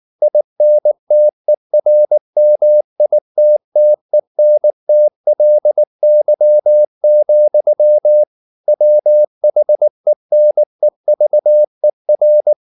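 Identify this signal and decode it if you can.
Morse code, heard as a keyed tone, about 19 wpm, 600 Hz. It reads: INTERMITTENTLY, WHENEVER
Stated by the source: Morse code at 19 wpm from CW med Gnister, reading Ken Follett's novel Never in English.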